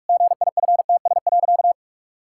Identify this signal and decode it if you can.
Morse code, heard as a keyed tone, about 48 wpm, 700 Hz. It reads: GIFTS.
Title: Pangram Challenge v2 - 45wpm